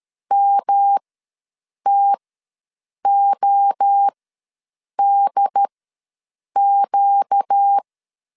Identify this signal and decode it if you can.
Morse code, heard as a keyed tone, about 13 wpm, spaced as for 7 wpm, 800 Hz. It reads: MTODQ
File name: LEZIONI / 6 PKX0 / 6 PKX0 ws 12wpm 3448